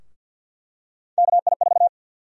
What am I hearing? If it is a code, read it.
Morse code, heard as a keyed tone, about 51 wpm, 700 Hz. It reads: KI4
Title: USA Call Sign Prefixes - 50wpm